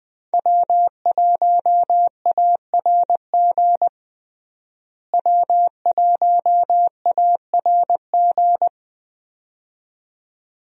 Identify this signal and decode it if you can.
Morse code, heard as a keyed tone, about 20 wpm, 700 Hz. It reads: W1ARG W1ARG
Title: USA Call Signs - 5 Characters - Repeated 2x - 20wpm